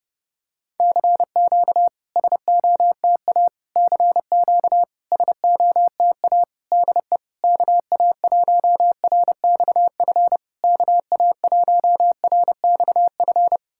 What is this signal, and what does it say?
Morse code, heard as a keyed tone, about 30 wpm, 660 Hz. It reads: CQ SOTA CQ SOTA DE KA1RXF KA1RXF
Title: SOTA QSO Practice 30wpm